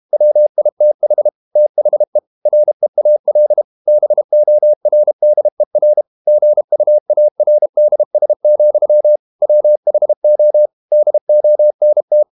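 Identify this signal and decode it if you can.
Morse code, heard as a keyed tone, about 32 wpm, 600 Hz. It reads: WITH THE REAL BORDER GUARDS, WHO DONT
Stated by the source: Morse code practice at 32 wpm from CW med Gnister, reading Ken Follett's novel Never in English.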